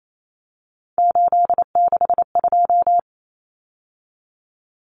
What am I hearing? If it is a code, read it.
Morse code, heard as a keyed tone, about 28 wpm, 700 Hz. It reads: 862
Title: Numbers - 3-Digit 28wpm